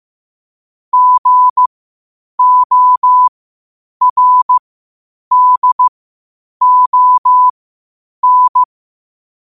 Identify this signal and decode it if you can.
Morse code, heard as a keyed tone, about 15 wpm, spaced as for 8 wpm, 1000 Hz. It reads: GORDON
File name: qotdfiles / 8wpm